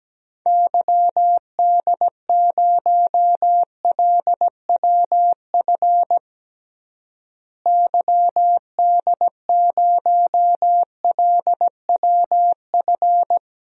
Morse code, heard as a keyed tone, about 17 wpm, 700 Hz. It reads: YD0LWF YD0LWF
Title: DXCC-100 Call Signs - 6 Characters - Repeated 3x - 17wpm